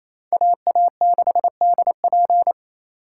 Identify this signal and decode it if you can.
Morse code, heard as a keyed tone, about 35 wpm, 700 Hz. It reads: AA6DP